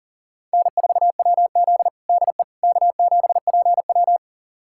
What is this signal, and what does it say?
Morse code, heard as a keyed tone, about 40 wpm, 700 Hz. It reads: N4WZ DE K7PW
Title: USA Callsign DE Callsign - 4 Characters - 40wpm